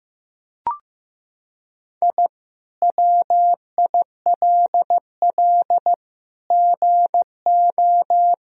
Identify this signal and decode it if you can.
Morse code, heard as a keyed tone, about 15 wpm, 700 Hz. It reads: I WILL GO
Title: Sentences from Top 1000 Words - Review Part 4 - 15wpm